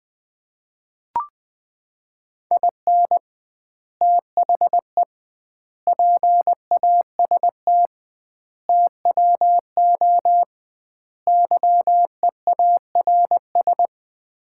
Morse code, heard as a keyed tone, about 20 wpm, 700 Hz. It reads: IN THE PAST TWO YEARS
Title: 2x Word-Spacing - Sets of 5 Words - 20wpm